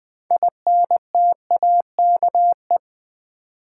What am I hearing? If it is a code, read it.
Morse code, heard as a keyed tone, about 20 wpm, 700 Hz. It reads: INTAKE
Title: 6 Letter Words - First 3 Letters Given - 20wpm